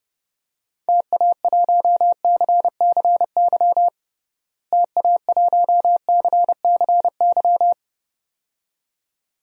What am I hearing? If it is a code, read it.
Morse code, heard as a keyed tone, about 30 wpm, 700 Hz. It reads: TA1CCY TA1CCY
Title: DXCC-100 Call Signs - 6 Characters - Repeated 2x - 30wpm